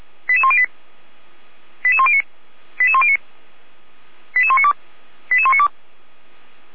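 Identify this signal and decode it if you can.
Five-tone selective call (ZVEI), heard as quick runs of five tones, tones of about 70 ms. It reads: 88108 88109 88109 88192 88192
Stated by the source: ZVEI five-tone selective call, code 88105, received by SDR; fire brigade Feuerwehr Orlamünde.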